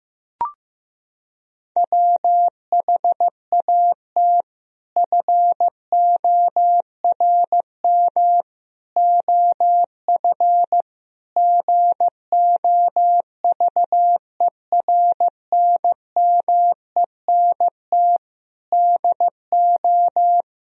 Morse code, heard as a keyed tone, about 15 wpm, 700 Hz. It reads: WHAT FORM OF GOVERNMENT DO
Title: Sentences from Top 1000 Words - Review Part 3 - 15wpm